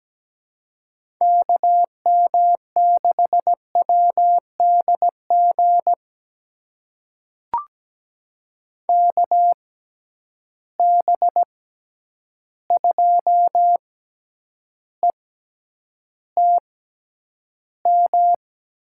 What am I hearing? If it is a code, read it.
Morse code, heard as a keyed tone, about 17 wpm, spaced as for 9 wpm, 700 Hz. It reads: KM6WDG KB2ETM
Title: USA Call Signs - 6 Characters - 6x Character Spacing - 17wpm